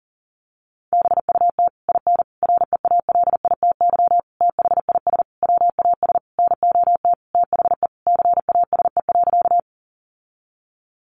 Morse code, heard as a keyed tone, about 40 wpm, 700 Hz. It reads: BUT IN REALITY THIS WAS NOT THE CASE.
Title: Sentences from Top 1200 Words 40wpm